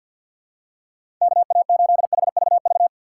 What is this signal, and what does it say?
Morse code, heard as a keyed tone, about 52 wpm, 700 Hz. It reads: KA8HVV
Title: USA Call Signs - 3x Character Spacing - 50wpm